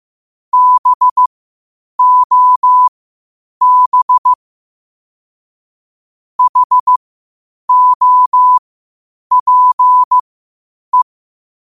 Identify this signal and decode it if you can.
Morse code, heard as a keyed tone, about 15 wpm, spaced as for 8 wpm, 1000 Hz. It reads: BOB HOPE
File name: qotdfiles / 8wpm